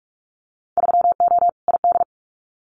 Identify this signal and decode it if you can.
Morse code, heard as a keyed tone, about 46 wpm, 700 Hz. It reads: 3O ID
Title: Winter Field Day Exchanges Encore 45wpm